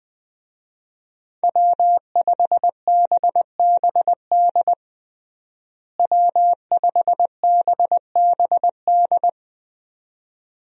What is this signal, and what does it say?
Morse code, heard as a keyed tone, about 20 wpm, 700 Hz. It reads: W5BBD W5BBD
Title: USA Call Signs - 5 Characters - Repeated 2x - 20wpm